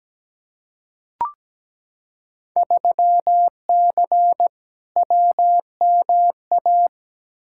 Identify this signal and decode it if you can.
Morse code, heard as a keyed tone, about 17 wpm, 700 Hz. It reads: 3C WMA
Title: Field Day Exchanges Encore - 17wpm